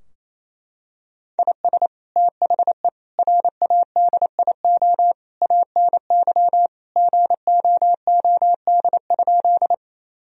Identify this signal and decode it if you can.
Morse code, heard as a keyed tone, about 28 wpm, 700 Hz. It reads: IS THE RADIO ANY GOOD?
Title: Sentences from Top 1000 Words - Review Part 3 - 28wpm